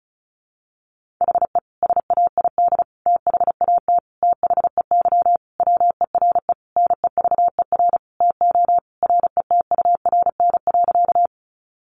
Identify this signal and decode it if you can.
Morse code, heard as a keyed tone, about 34 wpm, 700 Hz. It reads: HE SAID THAT THEY WERE NEVER TO RETURN.
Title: Sentences from Top 800 Words 35wpm